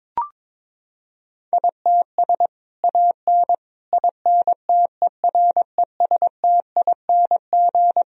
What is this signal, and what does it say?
Morse code, heard as a keyed tone, about 22 wpm, 700 Hz. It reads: ITS AN INTERESTING
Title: Sentences from Top 1000 Words - Review Part 4 - 22wpm